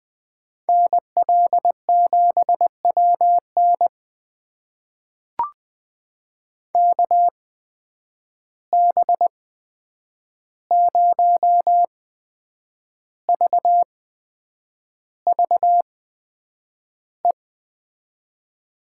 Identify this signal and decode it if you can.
Morse code, heard as a keyed tone, about 20 wpm, spaced as for 8 wpm, 700 Hz. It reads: NL7WN KB0VVE
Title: USA Call Signs - 8x Character Spacing - 20wpm